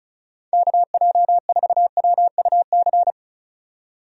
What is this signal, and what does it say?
Morse code, heard as a keyed tone, about 36 wpm, 700 Hz. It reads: KJ4WUC
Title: USA Call Signs - 6 Characters - 35wpm